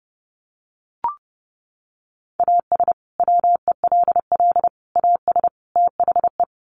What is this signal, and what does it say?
Morse code, heard as a keyed tone, about 30 wpm, 700 Hz. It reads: AS WELL AS THE
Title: Sets of 4 Words 30wpm